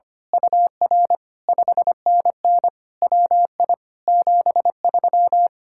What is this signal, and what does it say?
Morse code, heard as a keyed tone, about 24 wpm, 700 Hz. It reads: UR 5NN WI 73
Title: Speed-Racing Common QSO Phrases - 25wpm